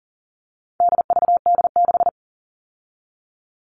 Morse code, heard as a keyed tone, about 40 wpm, 700 Hz. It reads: DVD6
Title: All Cut Numbers - 4-Digit 40wpm